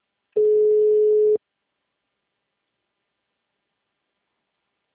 Ringback tone (Europe, 425 Hz).